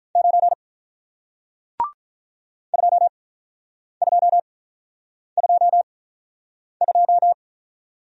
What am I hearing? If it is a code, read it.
Morse code, heard as a keyed tone, about 48 wpm, spaced as for 9 wpm, 700 Hz. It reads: 9 2222